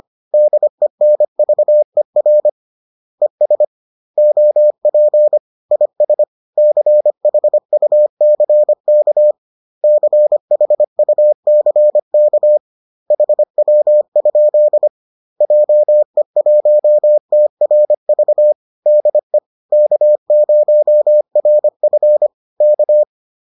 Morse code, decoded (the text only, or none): DENVER ES OP IS CHUCK CHUCK HW? JE1TRV DE K0RF K